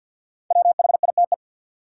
Morse code, heard as a keyed tone, about 53 wpm, 700 Hz. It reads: WHITE
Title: Sets of 2 Words - New Word Drill - 50wpm